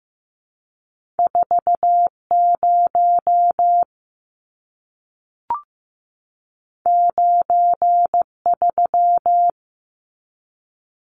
Morse code, heard as a keyed tone, about 15 wpm, 700 Hz. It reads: 40 93
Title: Numbers - 2-Digit 15wpm